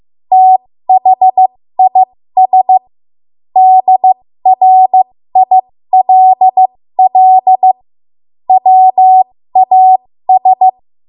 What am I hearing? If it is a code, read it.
Morse code, heard as a keyed tone, about 15 wpm, spaced as for 13 wpm, 750 Hz. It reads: THIS DRILL WAS
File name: Morse / 13WPM